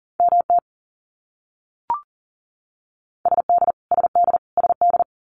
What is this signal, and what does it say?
Morse code, heard as a keyed tone, about 40 wpm, 700 Hz. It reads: MT SD SD SD
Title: US States Repeated 3x 40wpm